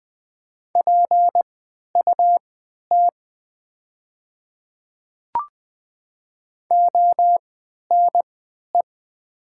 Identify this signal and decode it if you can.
Morse code, heard as a keyed tone, about 20 wpm, spaced as for 11 wpm, 700 Hz. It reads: PUT ONE
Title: 3x Character Spacing - Top 200 Words - 20wpm